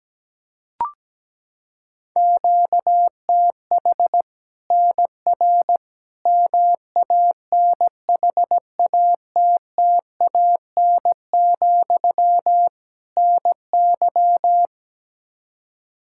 Morse code, heard as a keyed tone, about 17 wpm, 700 Hz. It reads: QTH NR MANHATTAN, NY
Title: Practice QSO - Exchange 1 - Beginning - 17wpm